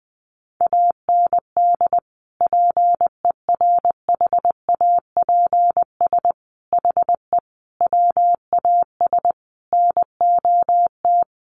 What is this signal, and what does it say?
Morse code, heard as a keyed tone, about 20 wpm, 700 Hz. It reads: AND PERHAPS HE WAS NOT
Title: Sentences from Top 900 Words 20wpm